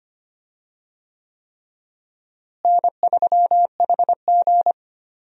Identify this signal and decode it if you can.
Morse code, heard as a keyed tone, about 25 wpm, 700 Hz. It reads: N3HG
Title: USA Call Signs - 4 Characters - Repeated 3x - 25wpm